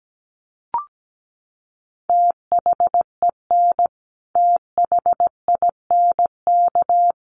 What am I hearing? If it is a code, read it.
Morse code, heard as a keyed tone, about 17 wpm, 700 Hz. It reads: THEN THINK
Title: Sentences from Top 1000 Words 17wpm